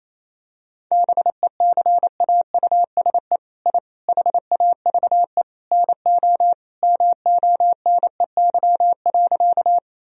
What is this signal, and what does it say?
Morse code, heard as a keyed tone, about 28 wpm, 700 Hz. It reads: BECAUSE I HAVE NO MONEY.
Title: Sentences from Top 1000 Words - Review Part 3 - 28wpm